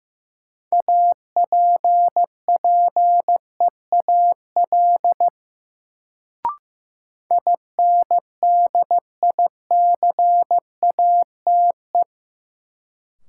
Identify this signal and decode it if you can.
Morse code, heard as a keyed tone, about 15 wpm, 700 Hz. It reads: APPEAL INDICATE
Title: Sentences from Top 1300 Words - New Word Drill Alt - 15wpm